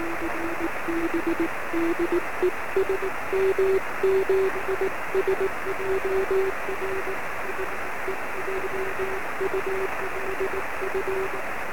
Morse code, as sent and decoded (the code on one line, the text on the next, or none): none